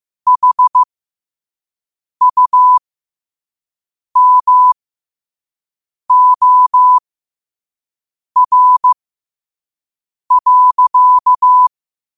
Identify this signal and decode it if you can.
Morse code, heard as a keyed tone, about 15 wpm, spaced as for 5 wpm, 1000 Hz. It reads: HUMOR.
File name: qotdfiles / 5wpm